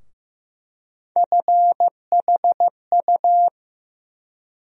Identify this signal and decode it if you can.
Morse code, heard as a keyed tone, about 15 wpm, 700 Hz. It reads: FHU
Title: USA Call Sign Postfixes - 3 Letters 15wpm